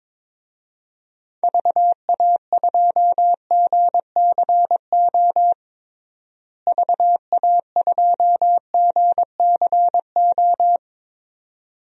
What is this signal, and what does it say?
Morse code, heard as a keyed tone, about 22 wpm, 700 Hz. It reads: VA2GCO VA2GCO
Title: DXCC-100 Call Signs - 6 Characters - Repeated 2x - 22wpm